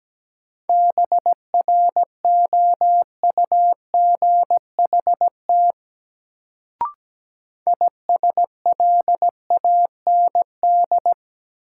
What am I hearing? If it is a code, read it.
Morse code, heard as a keyed tone, about 17 wpm, 700 Hz. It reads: BROUGHT ISLAND
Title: Sentences from Top 500 Words - New Word Drill - 17wpm